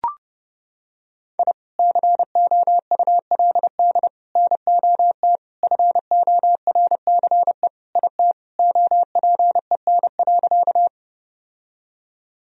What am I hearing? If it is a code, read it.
Morse code, heard as a keyed tone, about 32 wpm, 700 Hz. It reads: I COULD NOT FORCE IT OPEN.
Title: Sentences from Top 1000 Words - Review Part 3 - 30wpm